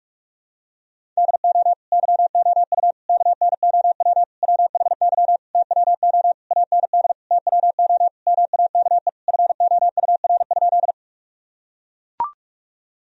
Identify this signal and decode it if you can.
Morse code, heard as a keyed tone, about 42 wpm, 700 Hz. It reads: DO YOU KNOW WHY TWO AND TWO MAKE FOUR?
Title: Sentences from Top 1000 Words - Review Part 1 - 45wpm